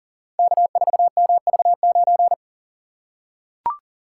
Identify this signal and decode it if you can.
Morse code, heard as a keyed tone, about 40 wpm, 700 Hz. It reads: K4MV9
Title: Code Groups - 5 Letter:Numbers - 40wpm